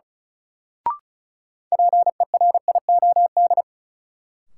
Morse code, heard as a keyed tone, about 36 wpm, 700 Hz. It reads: PERIOD
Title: Sentences from Top 800 Words - New Word Drill - 35wpm